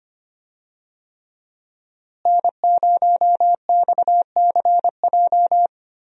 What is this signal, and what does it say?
Morse code, heard as a keyed tone, about 25 wpm, 700 Hz. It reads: N0XCJ
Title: USA Call Signs - 5 Characters - Repeated 3x - 25wpm